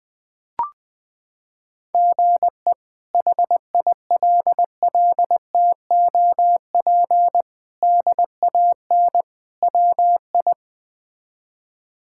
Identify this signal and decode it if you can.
Morse code, heard as a keyed tone, about 21 wpm, 700 Hz. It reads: GE HILLTOP DAN WI
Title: K1USN SST QSO Practice - 20wpm